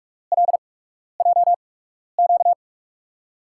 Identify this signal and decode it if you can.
Morse code, heard as a keyed tone, about 46 wpm, spaced as for 12 wpm, 700 Hz. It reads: LJQ
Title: USA Call Signs - 5 Characters - 8x Character Spacing - 45wpm